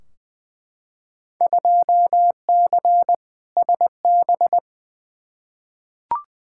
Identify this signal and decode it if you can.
Morse code, heard as a keyed tone, about 21 wpm, 700 Hz. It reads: 2C SB
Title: Field Day Exchanges Encore - 20wpm